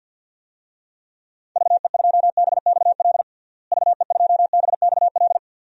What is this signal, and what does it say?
Morse code, heard as a keyed tone, about 48 wpm, 700 Hz. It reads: VE2BXL VE2BXL
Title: DXCC-100 Call Signs - 6 Characters - Repeated 2x - 50wpm